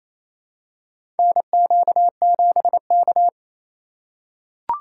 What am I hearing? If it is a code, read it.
Morse code, heard as a keyed tone, about 27 wpm, 700 Hz. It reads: NQ7K
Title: USA Call Signs - 2x Character Spacing - 28wpm